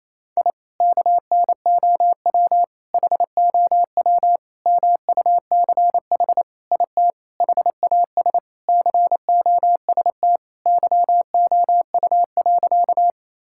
Morse code, heard as a keyed tone, about 28 wpm, 700 Hz. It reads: I KNOW HOW MUCH IT HAS COST YOU.
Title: Sentences from Top 1000 Words - Review Part 5 - 28wpm